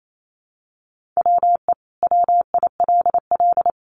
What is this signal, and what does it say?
Morse code, heard as a keyed tone, about 28 wpm, 700 Hz. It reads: WE WILL
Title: Sentences from Top 1100 Words 28wpm